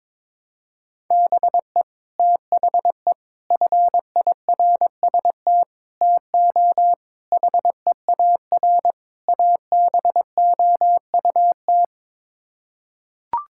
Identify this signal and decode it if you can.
Morse code, heard as a keyed tone, about 22 wpm, 700 Hz. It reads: BE THE FIRST TO HEAR ABOUT
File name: practice / 22wpm / Sets of 6 Words - 22wpm